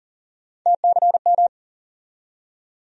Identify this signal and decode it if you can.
Morse code, heard as a keyed tone, about 40 wpm, 700 Hz. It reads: TCM
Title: USA Call Sign Postfixes - 3 Letters 40wpm